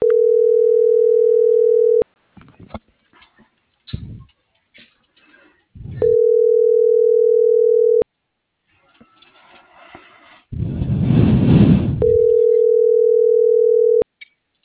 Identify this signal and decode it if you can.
Ringback tone (North America), 440 + 480 Hz.